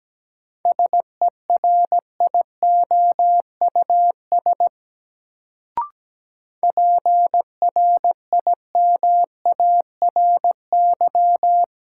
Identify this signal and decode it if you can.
Morse code, heard as a keyed tone, about 17 wpm, 700 Hz. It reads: SERIOUS PRIMARY
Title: Sentences from Top 1100 Words - New Word Drill Alt - 17wpm